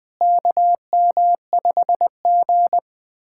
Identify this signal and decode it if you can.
Morse code, heard as a keyed tone, about 20 wpm, 700 Hz. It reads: KM5G
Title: USA Callsign DE Callsign - 4 Characters - 20wpm